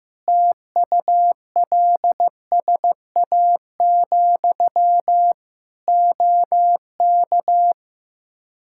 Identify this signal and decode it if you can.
Morse code, heard as a keyed tone, about 15 wpm, 700 Hz. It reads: TULSA, OK
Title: Practice QSO - Exchange 1 - Beginning - 15wpm